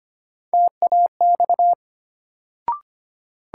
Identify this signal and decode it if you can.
Morse code, heard as a keyed tone, about 25 wpm, 700 Hz. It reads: TAX
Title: Sentences from Top 1000 Words - Review Part 5 - New Word Drill - 25wpm